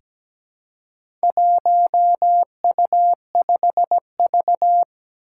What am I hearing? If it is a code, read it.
Morse code, heard as a keyed tone, about 17 wpm, 700 Hz. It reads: 1U5V
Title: Four Challenging Letter-Number Combinations 17wpm